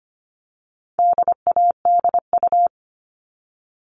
Morse code, heard as a keyed tone, about 25 wpm, 700 Hz. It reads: DADU
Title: All Cut Numbers - 4-Digit 25wpm